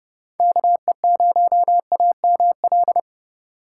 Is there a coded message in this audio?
Yes: Morse code from 0.4 to 3.0 s, heard as a keyed tone, about 29 wpm, 700 Hz. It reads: KE0AML